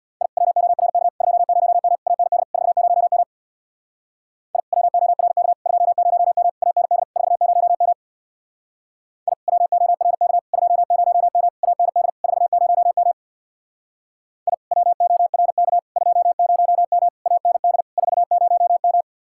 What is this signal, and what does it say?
Morse code, heard as a keyed tone, about 63 wpm, 700 Hz. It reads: I WORK 20M AND 40M I WORK 20M AND 40M I WORK 20M AND 40M I WORK 20M AND 40M